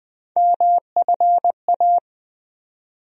Morse code, heard as a keyed tone, about 20 wpm, 700 Hz. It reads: MFA
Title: USA Call Sign Postfixes - 3 Letters 20wpm